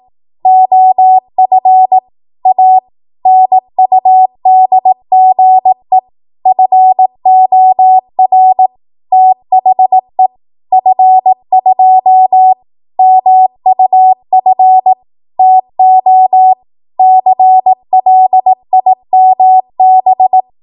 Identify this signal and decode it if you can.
Morse code, heard as a keyed tone, about 18 wpm, 750 Hz. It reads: OF A NUDGE FOR THE F2 MUF TO CLIMB
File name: Archive / 18 WPM / 18WPM